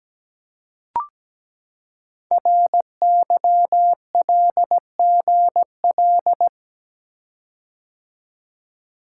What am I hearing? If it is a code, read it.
Morse code, heard as a keyed tone, about 17 wpm, 700 Hz. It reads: RYLGL
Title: Code Groups - 5 Letters - 17wpm